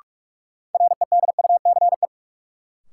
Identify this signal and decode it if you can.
Morse code, heard as a keyed tone, about 47 wpm, 700 Hz. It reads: REDUCE
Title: Sentences from Top 1100 Words - New Word Drill - 45wpm